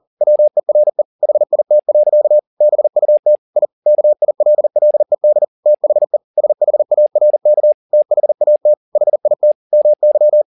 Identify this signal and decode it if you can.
Morse code, heard as a keyed tone, about 40 wpm, 600 Hz. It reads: WERE HIT. BUT I KILLED THE SHARK THAT HIT MY
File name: the old man and the sea 2 40wpm